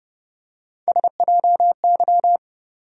Morse code, heard as a keyed tone, about 31 wpm, 700 Hz. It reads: SJY
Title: USA Call Sign Postfixes - 3 Letters 30wpm